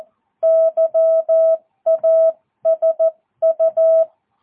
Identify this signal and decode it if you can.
Morse code, heard as a keyed tone, about 14 wpm, 650 Hz. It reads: YASU
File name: jh8cbh / 2022cw